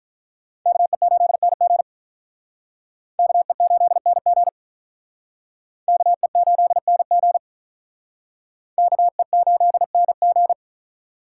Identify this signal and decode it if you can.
Morse code, heard as a keyed tone, about 43 wpm, 700 Hz. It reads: KE8NG KE8NG KE8NG KE8NG
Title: Speed Racing - USA 2x2 Call Signs 35wpm